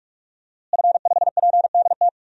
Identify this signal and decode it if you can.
Morse code, heard as a keyed tone, about 42 wpm, 700 Hz. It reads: F5PDT
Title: DXCC-100 Call Signs - 5 Characters - Repeated 2x - 45wpm